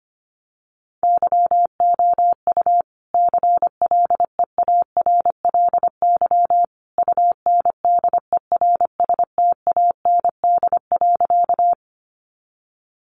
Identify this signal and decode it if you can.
Morse code, heard as a keyed tone, about 25 wpm, 700 Hz. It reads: YOU CLEARLY UNDERSTAND.
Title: Sentences from Top 1300 Words 25wpm